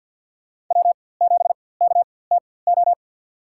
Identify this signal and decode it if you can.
Morse code, heard as a keyed tone, about 53 wpm, spaced as for 24 wpm, 700 Hz. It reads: W7KTO
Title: USA Call Signs - 5 Characters - 4x Character Spacing - 50wpm